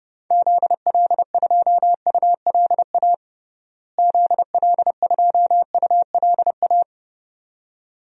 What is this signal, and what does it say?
Morse code, heard as a keyed tone, about 29 wpm, 700 Hz. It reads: ZL2ULA ZL2ULA
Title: DXCC-100 Call Signs - 6 Characters - Repeated 2x - 30wpm